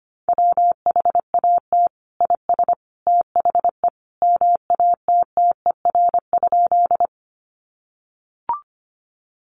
Morse code, heard as a keyed tone, about 25 wpm, 700 Hz. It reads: WHAT IS THE MATTER?
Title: Sentences from Top 700 Words 25wpm